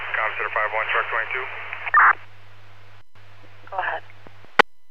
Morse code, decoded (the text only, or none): none